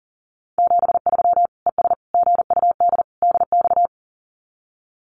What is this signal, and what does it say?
Morse code, heard as a keyed tone, about 40 wpm, 700 Hz. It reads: 73 ES GUD DX
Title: Common QSO Phrases 40wpm